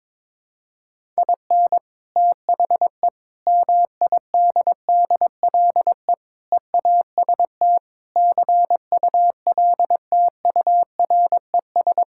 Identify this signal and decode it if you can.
Morse code, heard as a keyed tone, about 21 wpm, 700 Hz. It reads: IN THE MIDDLE EAST CULTURES
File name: practice / 22wpm / Sentences with 4-Word Phrases - 22wpm